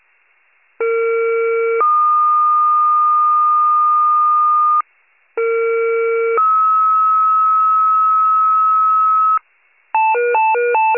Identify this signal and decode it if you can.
Two-tone page, heard as a one-second tone, then a three-second tone, about 460 then 1220 Hz, about 460 then 1290 Hz. It is heard from 0.8 to 9.4 s.
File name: pager / 21 RESCUE / wcalert